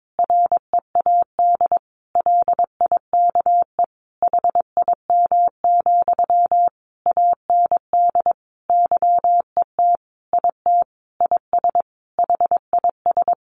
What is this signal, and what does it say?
Morse code, heard as a keyed tone, about 22 wpm, 700 Hz. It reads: READ LIKE HIM, AND YET IT IS HIS
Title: Sentences from Top 800 Words 22wpm